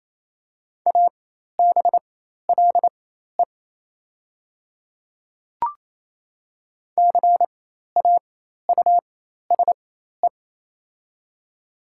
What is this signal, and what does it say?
Morse code, heard as a keyed tone, about 28 wpm, spaced as for 13 wpm, 690 Hz. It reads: ABLE CAUSE
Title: 4x Character Spacing - Top 500 Words - 28wpm